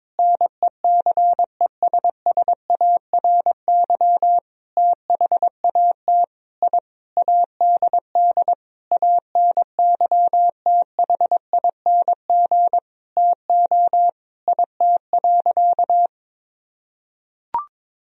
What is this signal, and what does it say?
Morse code, heard as a keyed tone, about 22 wpm, 700 Hz. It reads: NECESSARY THAT I ADD ANYTHING TO IT.
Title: Sentences from Top 1000 Words - Review Part 1 - 22wpm